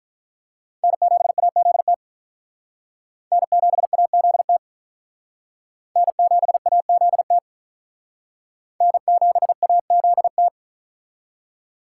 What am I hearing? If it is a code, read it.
Morse code, heard as a keyed tone, about 45 wpm, 700 Hz. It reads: N7AZT N7AZT N7AZT N7AZT